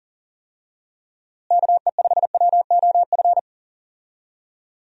Morse code, heard as a keyed tone, about 40 wpm, 700 Hz. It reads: KE5WOF
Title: USA Call Signs - 6 Characters - Repeated 3x - 40wpm